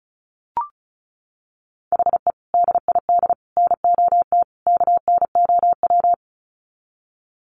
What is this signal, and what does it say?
Morse code, heard as a keyed tone, about 34 wpm, 700 Hz. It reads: HE DID NOT KNOW
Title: Sets of 4 Words 35wpm